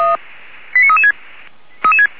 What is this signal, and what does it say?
Five-tone selective call (ZVEI), heard as quick runs of five tones, tones of about 70 ms. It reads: 89336
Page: /464/SDR/ZVEI/89325/